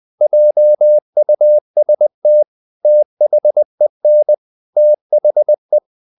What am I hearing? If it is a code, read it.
Morse code, heard as a keyed tone, about 20 wpm, 600 Hz. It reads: JUST THEN THE
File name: the old man and the sea part 2 20wpm